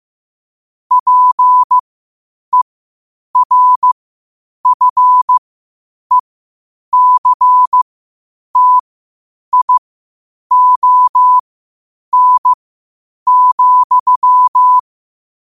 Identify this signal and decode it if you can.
Morse code, heard as a keyed tone, about 15 wpm, spaced as for 8 wpm, 1000 Hz. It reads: PERFECTION,
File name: qotdfiles / 8wpm